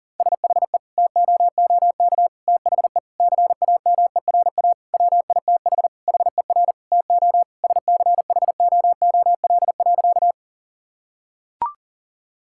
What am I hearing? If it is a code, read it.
Morse code, heard as a keyed tone, about 40 wpm, 700 Hz. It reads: SHE TOOK THE CAMERA WITH HER TO SCHOOL.
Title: Sentences from Top 1000 Words - Review Part 5 - 40wpm